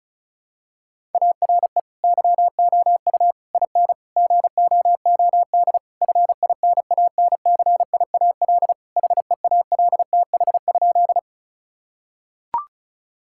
Morse code, heard as a keyed tone, about 34 wpm, 700 Hz. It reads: ARE YOU IN GOOD FINANCIAL HEALTH?
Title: Sentences from Top 1000 Words - Review Part 5 - 35wpm